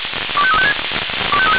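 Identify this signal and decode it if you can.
Five-tone selective call (ZVEI), heard as quick runs of five tones, tones of about 70 ms. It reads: 25356
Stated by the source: ZVEI five-tone selective call received on an SDR.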